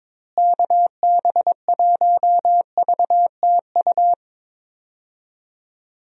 Morse code, heard as a keyed tone, about 22 wpm, 700 Hz. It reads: KB1VTU
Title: USA Call Signs - 6 Characters - Repeated 3x - 22wpm